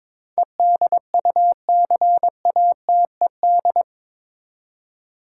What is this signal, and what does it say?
Morse code, heard as a keyed tone, about 22 wpm, 700 Hz. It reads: EDUCATED
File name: practice / 22wpm / Eight Letter Words - 22wpm